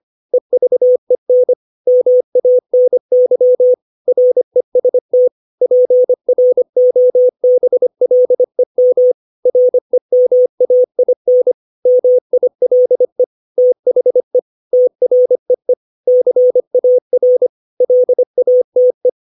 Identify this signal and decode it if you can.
Morse code, heard as a keyed tone, about 25 wpm, 500 Hz. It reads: EVEN MANY REST PROBLEM REMAIN MILE THE TREE CAR LATE
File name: 632 words / 632 Words 25WPM